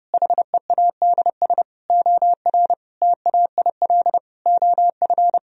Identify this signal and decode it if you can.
Morse code, heard as a keyed tone, about 28 wpm, 700 Hz. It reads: HEADS OR TAIL OF
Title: Sentences from Top 1000 Words - Review Part 3 - 30wpm